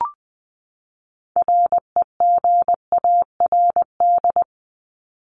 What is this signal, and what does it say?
Morse code, heard as a keyed tone, about 20 wpm, 700 Hz. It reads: REGARD
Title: ED Suffix - Top 100 Words with Base Words 20wpm